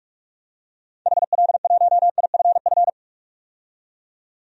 Morse code, heard as a keyed tone, about 46 wpm, 700 Hz. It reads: HL1IFF
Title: DXCC-100 Call Signs - 6 Characters - 45wpm